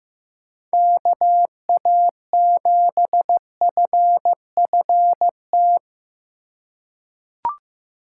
Morse code, heard as a keyed tone, about 15 wpm, 700 Hz. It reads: KA7FFT
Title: USA Call Signs - 6 Characters - 4x Character Spacing - 15wpm